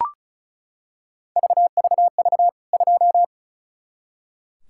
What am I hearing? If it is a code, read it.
Morse code, heard as a keyed tone, about 34 wpm, 700 Hz. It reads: VVV 2
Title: K1USN SST QSO Practice 35wpm